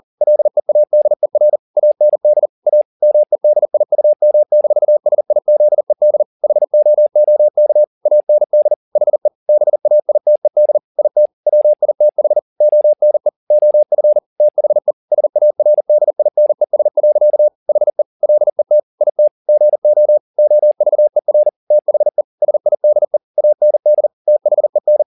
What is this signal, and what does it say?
Morse code, heard as a keyed tone, about 40 wpm, 600 Hz. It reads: LEADER AND A MEDIUM-SIZED HOOK AND HE BAITED IT WITH ONE OF THE SARDINES. HE LET IT GO OVER THE SIDE AND THEN